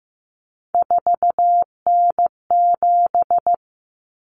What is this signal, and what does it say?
Morse code, heard as a keyed tone, about 15 wpm, 700 Hz. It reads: 4N7
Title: Common Cut Numbers - 3-Digit 15wpm